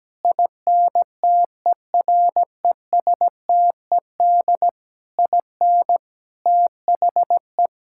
Morse code, heard as a keyed tone, about 17 wpm, 700 Hz. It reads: INTERESTED IN THE
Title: Sentences from Top 1000 Words - Review Part 6 - 17wpm